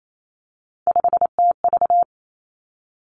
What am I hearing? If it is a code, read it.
Morse code, heard as a keyed tone, about 28 wpm, 700 Hz. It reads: 5TV
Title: Three Challenging Letter-Number Combinations 28wpm